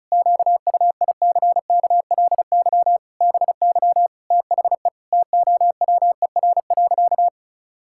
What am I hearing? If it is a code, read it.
Morse code, heard as a keyed tone, about 34 wpm, 700 Hz. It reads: QUICKLY BY THE TOWER.
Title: Pangram Challenge v2 - 35wpm